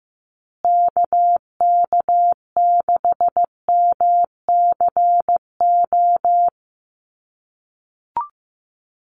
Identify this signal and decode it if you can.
Morse code, heard as a keyed tone, about 15 wpm, 700 Hz. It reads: KK6MCO
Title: USA Call Signs Repeated 3x 15wpm